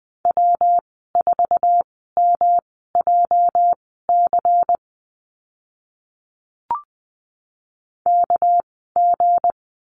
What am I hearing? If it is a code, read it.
Morse code, heard as a keyed tone, about 20 wpm, spaced as for 15 wpm, 700 Hz. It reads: W4MJC KG
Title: 219 Lesson 35g - Number 0 - Call Signs 20wpm